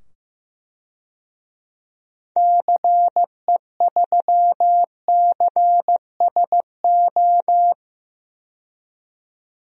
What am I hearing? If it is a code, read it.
Morse code, heard as a keyed tone, about 15 wpm, 700 Hz. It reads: CE3CSO
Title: DXCC-100 Call Signs - 6 Characters - 15wpm